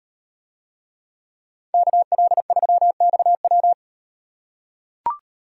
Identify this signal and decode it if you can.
Morse code, heard as a keyed tone, about 36 wpm, 700 Hz. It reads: KL3XW